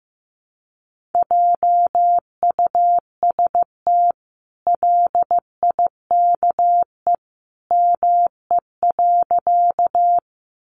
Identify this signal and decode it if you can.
Morse code, heard as a keyed tone, about 15 wpm, 700 Hz. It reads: JUST LIKE ME.
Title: Sentences from Top 200 Words 15wpm